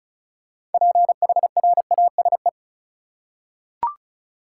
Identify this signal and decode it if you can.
Morse code, heard as a keyed tone, about 34 wpm, 700 Hz. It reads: PHRASE
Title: Six Letter Words - 35wpm